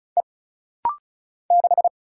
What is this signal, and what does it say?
Morse code, heard as a keyed tone, about 34 wpm, 700 Hz. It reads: E 6
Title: Single Letter-Number Alt - Rapid-Fire 35wpm